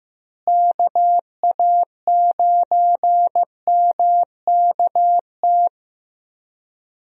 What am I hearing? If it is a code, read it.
Morse code, heard as a keyed tone, about 15 wpm, 700 Hz. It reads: KA9MKT
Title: USA Call Signs - 6 Characters - Repeated 2x - 15wpm